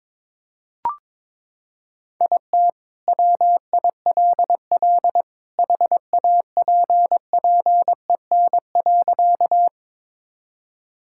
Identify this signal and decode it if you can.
Morse code, heard as a keyed tone, about 22 wpm, 700 Hz. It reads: IT WILL HAPPEN.
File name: practice / 22wpm / Sentences with 2-Word Phrases - 22wpm